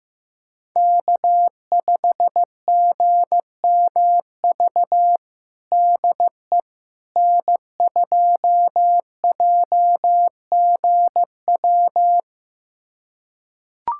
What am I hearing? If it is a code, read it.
Morse code, heard as a keyed tone, about 16 wpm, 700 Hz. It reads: K5GMV DE N2JGW
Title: USA Callsign DE Callsign - 5 Characters - 15wpm